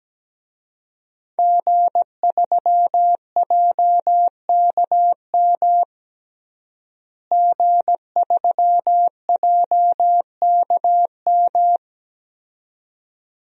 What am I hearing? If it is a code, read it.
Morse code, heard as a keyed tone, about 17 wpm, 700 Hz. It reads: G3JKM G3JKM
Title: DXCC-100 Call Signs - 5 Characters - Repeated 2x - 17wpm